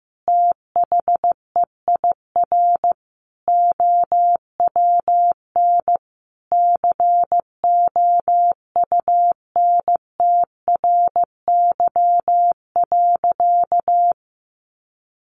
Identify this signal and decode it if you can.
Morse code, heard as a keyed tone, about 15 wpm, 700 Hz. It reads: THEIR OWN COUNTRY.
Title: Sentences from Top 300 Words 15wpm